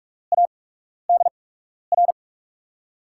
Morse code, heard as a keyed tone, about 44 wpm, spaced as for 12 wpm, 690 Hz. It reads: ADR
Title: USA Call Signs - 5 Characters - 8x Character Spacing - 45wpm